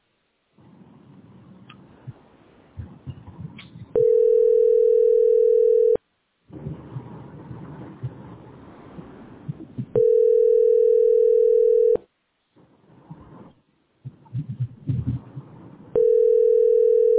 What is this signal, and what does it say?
Ringback tone (North America), 440 + 480 Hz.